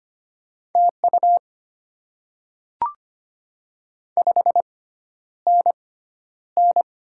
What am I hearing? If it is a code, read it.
Morse code, heard as a keyed tone, about 25 wpm, spaced as for 11 wpm, 700 Hz. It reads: TU 5NN
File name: Top 20 QSO Elements - 6x Character Spacing - 25wpm